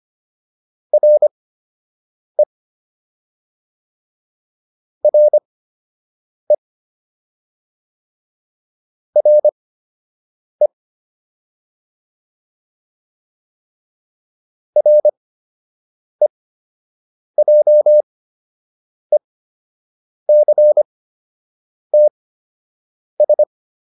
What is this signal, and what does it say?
Morse code, heard as a keyed tone, about 25 wpm, spaced as for 7 wpm, 600 Hz. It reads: RE RE RE REJECTS